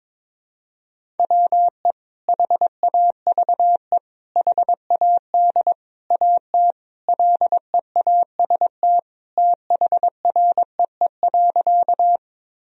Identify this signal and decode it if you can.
Morse code, heard as a keyed tone, about 22 wpm, 700 Hz. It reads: WE HAVE HAD AT LEAST THREE.
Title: Sentences from Top 1000 Words - Review Part 1 - 22wpm